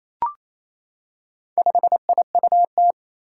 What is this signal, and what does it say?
Morse code, heard as a keyed tone, about 28 wpm, 700 Hz. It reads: 5IUT
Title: Four Challenging Letter-Number Combinations 28wpm